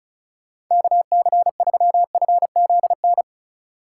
Morse code, heard as a keyed tone, about 37 wpm, 700 Hz. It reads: KC3FZN
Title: Speed Racing - USA 2x3 Call Signs 30wpm